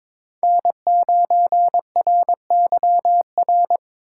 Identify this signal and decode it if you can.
Morse code, heard as a keyed tone, about 22 wpm, 700 Hz. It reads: N9RYR
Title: USA Call Signs - 5 Characters - 2x Character Spacing - 22wpm